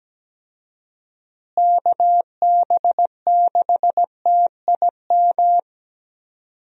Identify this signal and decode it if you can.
Morse code, heard as a keyed tone, about 17 wpm, 700 Hz. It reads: KB6TIM
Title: USA Call Signs - 6 Characters - Repeated 3x - 17wpm